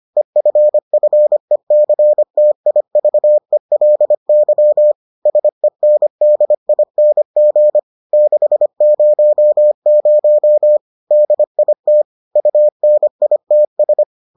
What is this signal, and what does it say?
Morse code, heard as a keyed tone, about 25 wpm, 600 Hz. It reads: EFFECTIVELY SENDING 600 DIT UNITS